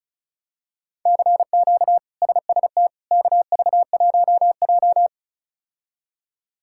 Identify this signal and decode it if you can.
Morse code, heard as a keyed tone, about 36 wpm, 700 Hz. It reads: CQ SST KV1J
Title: K1USN SST QSO Practice - 35wpm